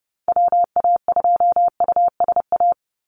Morse code, heard as a keyed tone, about 30 wpm, 700 Hz. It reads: WA2USA
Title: NA SOTA Chaser Callsigns 30wpm